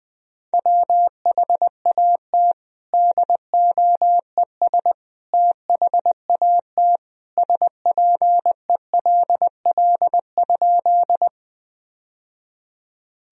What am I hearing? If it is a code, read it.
Morse code, heard as a keyed tone, about 20 wpm, 700 Hz. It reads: WHAT DOES THAT SPELL?
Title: Sentences from Top 1000 Words - Review Part 1 - 20wpm